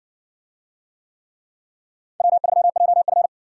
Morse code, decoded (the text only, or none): W3JF